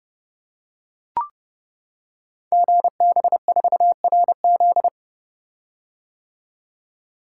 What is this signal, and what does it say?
Morse code, heard as a keyed tone, about 30 wpm, 690 Hz. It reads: GB4RZ
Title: Code Groups - 5 Letter:Numbers - 30wpm